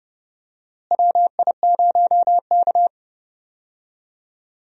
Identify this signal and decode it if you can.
Morse code, heard as a keyed tone, about 31 wpm, 700 Hz. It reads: WI0K
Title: USA Call Signs - 4 Characters - Repeated 2x - 30wpm